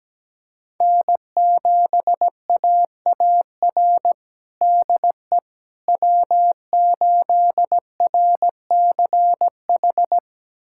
Morse code, heard as a keyed tone, about 17 wpm, 700 Hz. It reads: N7AAR DE W8RCH
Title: USA Callsign DE Callsign - 5 Characters - 17wpm